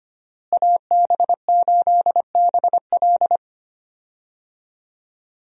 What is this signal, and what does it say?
Morse code, heard as a keyed tone, about 25 wpm, 700 Hz. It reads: AB8BL
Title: USA Call Signs - 5 Characters - Repeated 3x - 25wpm